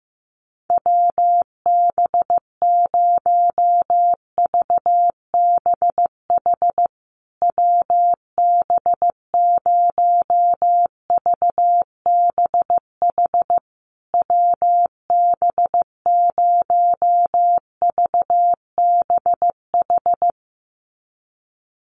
Morse code, heard as a keyed tone, about 15 wpm, 700 Hz. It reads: WB0VBH WB0VBH WB0VBH
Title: USA Call Signs Repeated 3x 15wpm